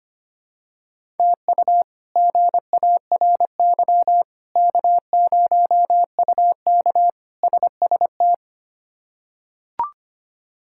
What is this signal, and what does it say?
Morse code, heard as a keyed tone, about 25 wpm, 700 Hz. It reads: TU GARY K0UK SST
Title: K1USN SST QSO Practice - 25wpm